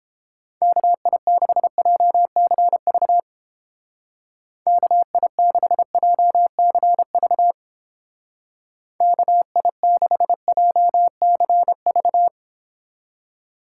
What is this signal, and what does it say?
Morse code, heard as a keyed tone, about 29 wpm, 700 Hz. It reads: KI6JCV KI6JCV KI6JCV